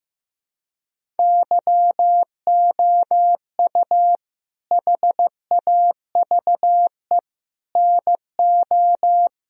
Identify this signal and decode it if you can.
Morse code, heard as a keyed tone, about 15 wpm, 700 Hz. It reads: YOU HAVE NO
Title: Sentences from Top 1000 Words - Review Part 1 - 15wpm